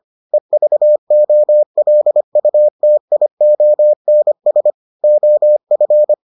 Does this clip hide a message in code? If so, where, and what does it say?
Yes: Morse code from 0.3 to 6.1 s, heard as a keyed tone, about 25 wpm, 600 Hz. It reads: EVOLUTIONS OF